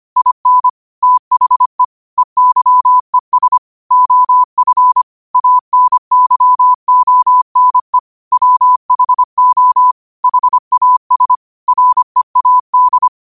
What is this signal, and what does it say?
Morse code, heard as a keyed tone, about 25 wpm, 1000 Hz. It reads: IN THE EYES OF ANYONE WHO HAS READ
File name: qotdfiles / 25wpm